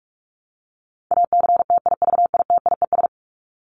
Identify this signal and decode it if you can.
Morse code, heard as a keyed tone, about 44 wpm, 700 Hz. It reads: ACTIVITIES
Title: ES:S Suffix - Top 100 Words with Base Words 45wpm